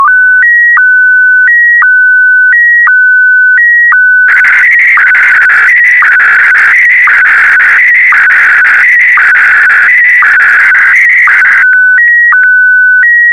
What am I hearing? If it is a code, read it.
SSTV (slow-scan television), Scottie DX.